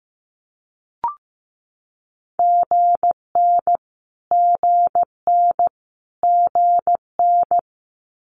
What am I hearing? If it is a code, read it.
Morse code, heard as a keyed tone, about 15 wpm, 700 Hz. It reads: GN GN GN
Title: Multi-letter Phonemes Repeated 3x 15wpm